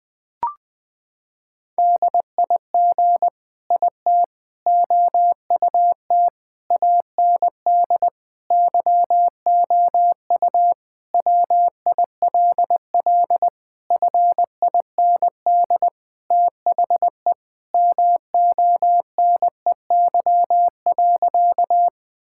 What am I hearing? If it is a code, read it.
Morse code, heard as a keyed tone, about 20 wpm, 700 Hz. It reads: DIG IT OUT AND YOU WILL FIND THE MONEY.
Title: Sentences from Top 1600 Words 20wpm